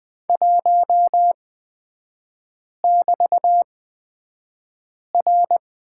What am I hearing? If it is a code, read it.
Morse code, heard as a keyed tone, about 20 wpm, spaced as for 5 wpm, 700 Hz. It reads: 1<BT>R
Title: 224 Lesson 36e - BT - Three Characters 20wpm